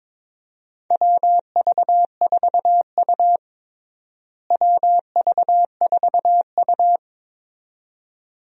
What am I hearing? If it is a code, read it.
Morse code, heard as a keyed tone, about 22 wpm, 700 Hz. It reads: WV4U WV4U
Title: USA Call Signs - 4 Characters - Repeated 2x - 22wpm